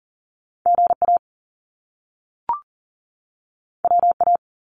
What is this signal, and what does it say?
Morse code, heard as a keyed tone, about 42 wpm, 700 Hz. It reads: GA WA
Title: US States 40wpm